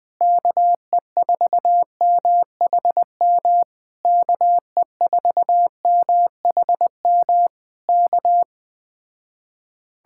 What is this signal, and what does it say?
Morse code, heard as a keyed tone, about 20 wpm, 700 Hz. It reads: KE4MHM KE4MHM K
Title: Speed-Racing Common QSO Phrases - 17wpm